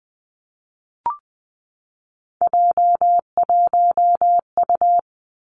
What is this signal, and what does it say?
Morse code, heard as a keyed tone, about 20 wpm, 700 Hz. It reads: J1U
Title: Three Challenging Letter-Number Combinations 20wpm